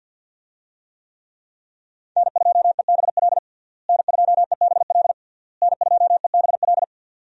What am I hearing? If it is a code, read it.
Morse code, heard as a keyed tone, about 51 wpm, 700 Hz. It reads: N2EBL N2EBL N2EBL
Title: USA Call Signs - 5 Characters - Repeated 3x - 50wpm